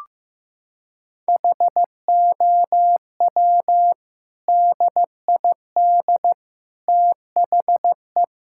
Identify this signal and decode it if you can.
Morse code, heard as a keyed tone, about 15 wpm, 700 Hz. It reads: HOW DID THE
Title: Sentences from Top 1000 Words - Review Part 6 - 15wpm